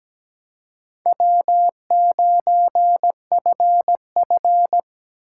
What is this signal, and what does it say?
Morse code, heard as a keyed tone, about 20 wpm, 700 Hz. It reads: W9FF
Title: Speed Racing - USA Call Signs 17wpm